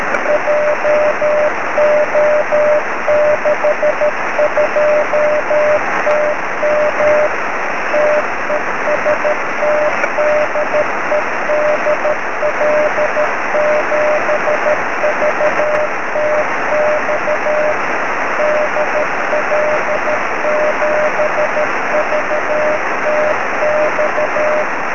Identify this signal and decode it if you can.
Morse code, heard as a keyed tone, about 13 wpm, 600 Hz, with noise. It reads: JO62TM TEST DE DL7VTX DL7VTX
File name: sm7lcb / 18-dl7vtx 23cm